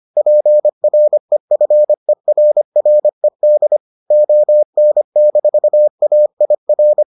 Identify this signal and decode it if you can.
Morse code, heard as a keyed tone, about 25 wpm, 600 Hz. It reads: PREFERRED ON-AIR